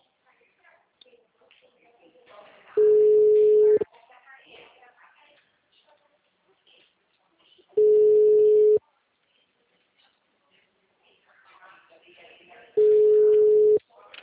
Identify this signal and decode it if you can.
Ringback tone (Europe, 425 Hz), with speech.